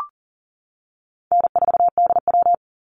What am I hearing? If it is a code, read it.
Morse code, heard as a keyed tone, about 40 wpm, 700 Hz. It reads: N4DW
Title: Call Signs - 40wpm